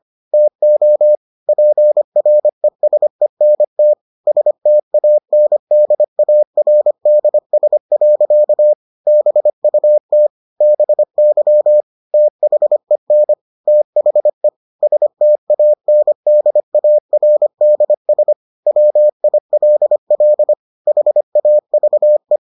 Morse code, heard as a keyed tone, about 25 wpm, 600 Hz. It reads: TO PRESENT STANDARDS. BUT BY THEN THE STANDARDS WILL HAVE